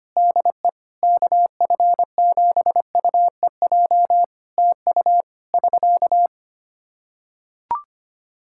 Morse code, heard as a keyed tone, about 25 wpm, 700 Hz. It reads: DE KF7UEJ TU <SK>
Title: K1USN SST QSO Practice 25wpm